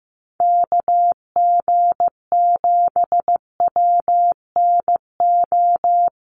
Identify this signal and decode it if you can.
Morse code, heard as a keyed tone, about 15 wpm, 700 Hz. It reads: KG7WNO
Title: USA Call Signs Repeated 2x 15wpm